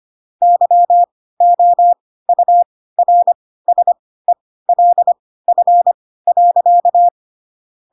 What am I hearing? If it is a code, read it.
Morse code, heard as a keyed tone, about 25 wpm, spaced as for 16 wpm, 700 Hz. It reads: YOURSELF.
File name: Pinocchio / Pinocchio ch5 10 25 ik1hge